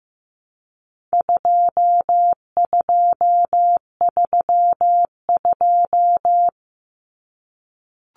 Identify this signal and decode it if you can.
Morse code, heard as a keyed tone, about 15 wpm, 700 Hz. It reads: 2232
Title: Common Cut Numbers - 4-Digit 15wpm